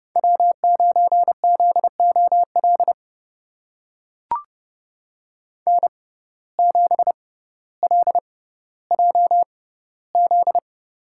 Morse code, heard as a keyed tone, about 30 wpm, spaced as for 15 wpm, 700 Hz. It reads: W9ZOL N7LJZ
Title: USA Call Signs - 6x Character Spacing - 30wpm